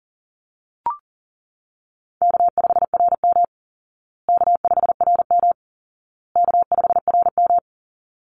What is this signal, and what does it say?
Morse code, heard as a keyed tone, about 40 wpm, 700 Hz. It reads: K5RM K5RM K5RM
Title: Easy Call Signs Repeated 3x - 40wpm